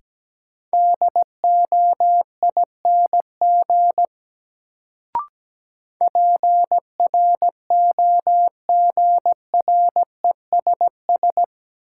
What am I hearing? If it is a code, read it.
Morse code, heard as a keyed tone, about 17 wpm, 700 Hz. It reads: DOING PROGRESS
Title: Sentences from Top 1400 Words - New Word Drill Alt - 17wpm